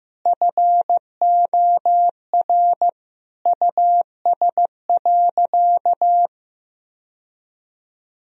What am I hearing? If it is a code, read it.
Morse code, heard as a keyed tone, about 15 wpm, 700 Hz. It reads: FOR US.
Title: Sentences from Top 1000 Words - Review Part 2 - 15wpm